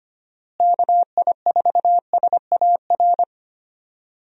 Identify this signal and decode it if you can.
Morse code, heard as a keyed tone, about 25 wpm, 700 Hz. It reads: KI4SAR
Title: USA Call Signs - 6 Characters - Repeated 3x - 25wpm